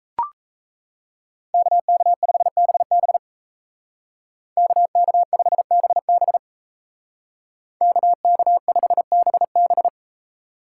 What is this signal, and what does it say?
Morse code, heard as a keyed tone, about 36 wpm, 700 Hz. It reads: KK5BB KK5BB KK5BB